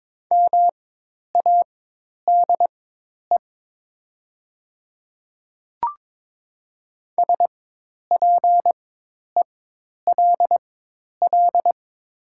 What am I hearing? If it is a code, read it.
Morse code, heard as a keyed tone, about 22 wpm, spaced as for 10 wpm, 700 Hz. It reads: MADE SPELL
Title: 4x Character Spacing - Top 200 Words - 22wpm